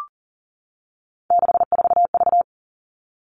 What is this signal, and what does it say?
Morse code, heard as a keyed tone, about 40 wpm, 700 Hz. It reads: 64V
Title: All Cut Numbers - 3-Digit 40wpm